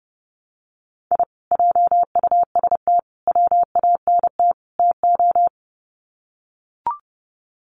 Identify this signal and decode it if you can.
Morse code, heard as a keyed tone, about 30 wpm, 700 Hz. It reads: I JUST WANT TO
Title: Sets of 4 Words 30wpm